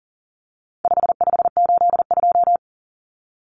Morse code, heard as a keyed tone, about 40 wpm, 700 Hz. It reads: 5582
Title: Numbers - 4-Digit 40wpm